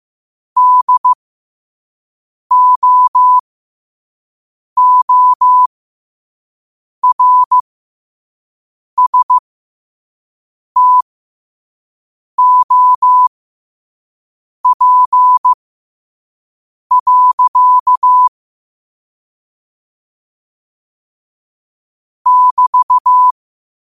Morse code, heard as a keyed tone, about 15 wpm, spaced as for 5 wpm, 1000 Hz. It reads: DOORSTOP. <BT>
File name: qotdfiles / 5wpm